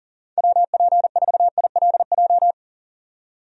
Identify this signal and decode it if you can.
Morse code, heard as a keyed tone, about 40 wpm, 700 Hz. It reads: WP4ILJ
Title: USA Call Signs - 6 Characters - 40wpm